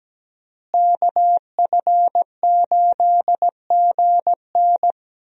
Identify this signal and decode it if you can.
Morse code, heard as a keyed tone, about 17 wpm, 700 Hz. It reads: KF8GN
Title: USA Call Signs - 5 Characters - Repeated 2x - 17wpm